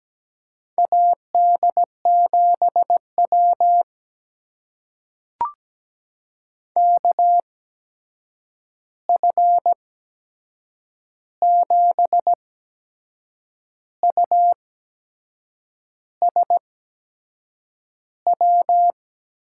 Morse code, heard as a keyed tone, about 17 wpm, spaced as for 6 wpm, 700 Hz. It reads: AD7W KF7USW
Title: USA Call Signs - 8x Character Spacing - 17wpm